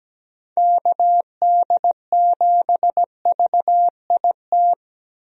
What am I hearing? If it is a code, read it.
Morse code, heard as a keyed tone, about 17 wpm, 700 Hz. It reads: KD7VIT
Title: USA Call Signs - 6 Characters - Repeated 2x - 17wpm